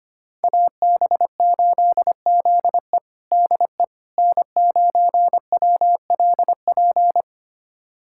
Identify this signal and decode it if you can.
Morse code, heard as a keyed tone, about 25 wpm, 700 Hz. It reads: AB8ZE DE N9WLP
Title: USA Callsign DE Callsign - 5 Characters - 25wpm